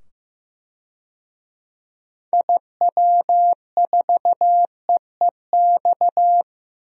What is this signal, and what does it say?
Morse code, heard as a keyed tone, about 14 wpm, 700 Hz. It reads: IW4EEX
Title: DXCC-100 Call Signs - 6 Characters - 15wpm